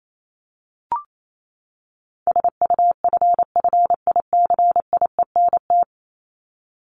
Morse code, heard as a keyed tone, about 28 wpm, 700 Hz. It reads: SUFFICIENT
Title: Top 1251-1500 Words 28wpm